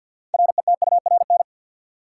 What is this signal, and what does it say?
Morse code, heard as a keyed tone, about 49 wpm, 700 Hz. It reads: RETURN
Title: Sentences from Top 800 Words - New Word Drill - 50wpm